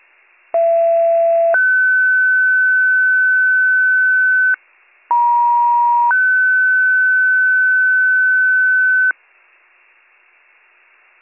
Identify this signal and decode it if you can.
Two-tone page, heard as a one-second tone, then a three-second tone, about 670 then 1500 Hz, about 950 then 1500 Hz. It is heard from 0.5 to 9.1 s.